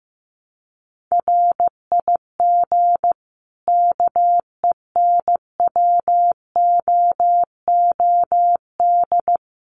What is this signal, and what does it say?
Morse code, heard as a keyed tone, about 15 wpm, 700 Hz. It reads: RIG KENWOOD
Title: Common QSO Phrases 15wpm